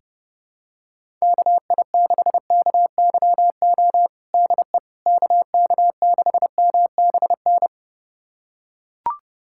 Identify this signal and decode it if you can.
Morse code, heard as a keyed tone, about 30 wpm, 700 Hz. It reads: KI6KYO DE KK6MBN
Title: USA Callsign DE Callsign - 6 Characters - 30wpm